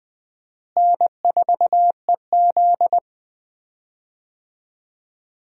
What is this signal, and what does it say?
Morse code, heard as a keyed tone, about 20 wpm, 700 Hz. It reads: N4EZ
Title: USA Call Signs - 4 Characters - Repeated 3x - 20wpm